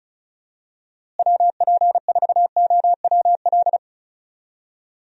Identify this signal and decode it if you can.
Morse code, heard as a keyed tone, about 36 wpm, 700 Hz. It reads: WP4OWL
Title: USA Call Signs - 6 Characters - Repeated 3x - 35wpm